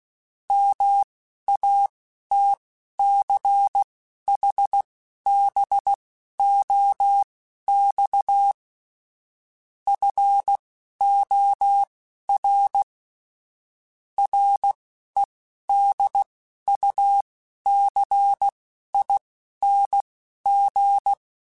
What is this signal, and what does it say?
Morse code, heard as a keyed tone, about 16 wpm, spaced as for 12 wpm, 800 Hz. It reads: MATCHBOX FOR REDUCING
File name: (spaced) / QSOs Spaced-mp3s / WD0FBT 1 @15 sp